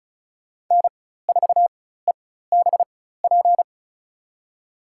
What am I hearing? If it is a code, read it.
Morse code, heard as a keyed tone, about 36 wpm, spaced as for 16 wpm, 700 Hz. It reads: N4EBP
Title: USA Call Signs - 4x Character Spacing - 35wpm